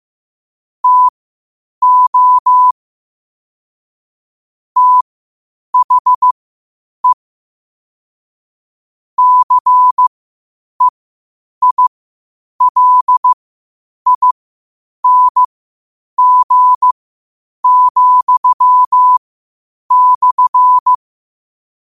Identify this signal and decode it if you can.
Morse code, heard as a keyed tone, about 15 wpm, spaced as for 8 wpm, 1000 Hz. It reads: TO THE CEILING,/
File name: qotdfiles / 8wpm